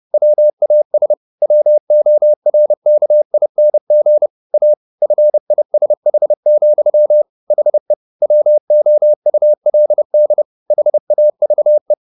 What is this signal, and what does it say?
Morse code, heard as a keyed tone, about 30 wpm, 600 Hz. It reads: WAS WORKING A FISH, HE WOULD HAVE
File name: the old man and the sea / the old man and the sea 1 30wpm